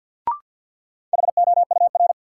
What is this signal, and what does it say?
Morse code, heard as a keyed tone, about 47 wpm, 700 Hz. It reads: HOUR
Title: Sets of 5 Words - New Word Drill - 50wpm